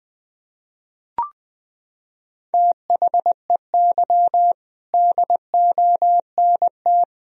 Morse code, heard as a keyed tone, about 20 wpm, 700 Hz. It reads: THEY DONT
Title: Sentences from Top 1000 Words - Review Part 2 - 20wpm